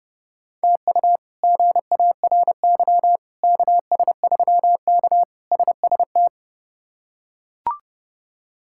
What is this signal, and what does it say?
Morse code, heard as a keyed tone, about 30 wpm, 700 Hz. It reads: TU GARY KS3K SST
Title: K1USN SST QSO Practice - 30wpm